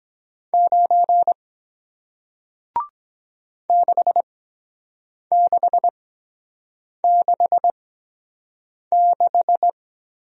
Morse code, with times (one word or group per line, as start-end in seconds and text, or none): 0.5-1.3 9
3.7-9.7 6666